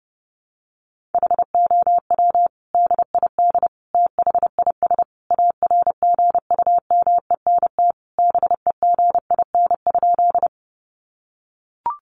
Morse code, heard as a keyed tone, about 30 wpm, 700 Hz. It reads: HOW DID THIS ARGUMENT BEGIN?
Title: Sentences from Top 1400 Words 30wpm